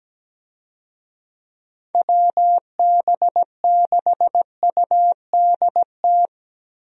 Morse code, heard as a keyed tone, about 17 wpm, 700 Hz. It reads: WB6UDT
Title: USA Call Signs - 6 Characters - Repeated 3x - 17wpm